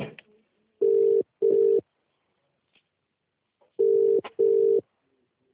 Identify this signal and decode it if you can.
Ringback tone (UK), 400 + 450 Hz.